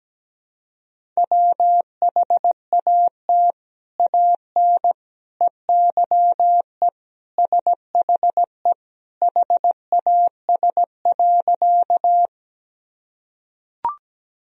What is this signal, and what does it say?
Morse code, heard as a keyed tone, about 17 wpm, 700 Hz. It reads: WHAT AN EYE SHE HAS.
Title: Sentences from Top 1000 Words - Review Part 1 - 17wpm